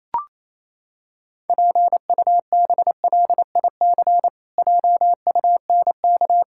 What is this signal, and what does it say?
Morse code, heard as a keyed tone, about 28 wpm, 700 Hz. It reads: PUBLIC JUNK
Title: Pangram Challenge v2 - 28wpm